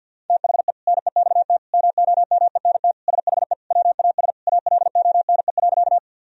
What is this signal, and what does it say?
Morse code, heard as a keyed tone, about 52 wpm, 700 Hz. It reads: THE NEXT MOMENT SHE WAS ALONE.
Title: Sentences from Top 1000 Words - Review Part 1 - 50wpm